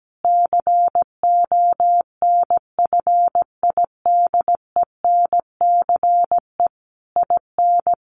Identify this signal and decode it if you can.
Morse code, heard as a keyed tone, about 17 wpm, 700 Hz. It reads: CONFIDENCE IN
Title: Sentences from Top 1500 Words 17wpm